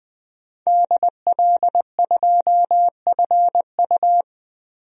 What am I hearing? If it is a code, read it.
Morse code, heard as a keyed tone, about 20 wpm, 700 Hz. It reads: DL2FU
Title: DXCC-100 Call Signs - 5 Characters - Repeated 2x - 20wpm